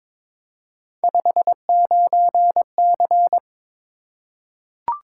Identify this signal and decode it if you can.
Morse code, heard as a keyed tone, about 22 wpm, 700 Hz. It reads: 59C
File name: practice / 22wpm / Three Letter-Number Combinations - 22wpm